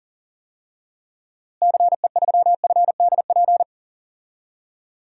Morse code, heard as a keyed tone, about 40 wpm, 700 Hz. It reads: CE3FDP
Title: DXCC-100 Call Signs - 6 Characters - Repeated 2x - 40wpm